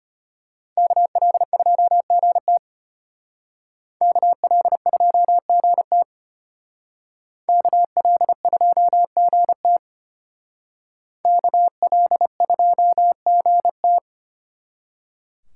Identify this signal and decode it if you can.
Morse code, heard as a keyed tone, about 32 wpm, 700 Hz. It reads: KL2GT KL2GT KL2GT KL2GT